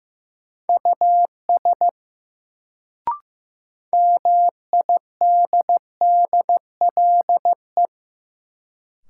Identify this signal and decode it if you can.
Morse code, heard as a keyed tone, about 15 wpm, 700 Hz. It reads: US MIDDLE
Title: Sets of 2 Words - New Word Drill - 15wpm